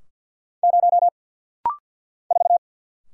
Morse code, heard as a keyed tone, about 50 wpm, 700 Hz. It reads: 0 4
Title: Single Letter-Number - Rapid-Fire 50wpm